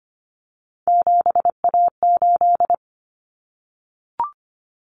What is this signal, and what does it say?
Morse code, heard as a keyed tone, about 25 wpm, 700 Hz. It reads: 7A8
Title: Common Cut Numbers - 3-Digit 25wpm